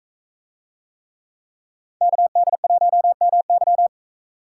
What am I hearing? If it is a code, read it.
Morse code, heard as a keyed tone, about 42 wpm, 700 Hz. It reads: KD1MY